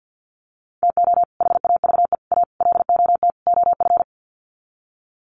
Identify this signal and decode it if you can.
Morse code, heard as a keyed tone, about 50 wpm, 700 Hz. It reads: TO HAVE A LOT OF
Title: Sets of 5 Words Encore 50wpm